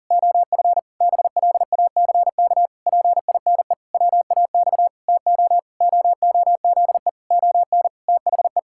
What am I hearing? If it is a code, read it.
Morse code, heard as a keyed tone, about 40 wpm, 700 Hz. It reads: OF BLACK PINE WAX TO OOZE ON THE
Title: Pangram Challenge v2 - 40wpm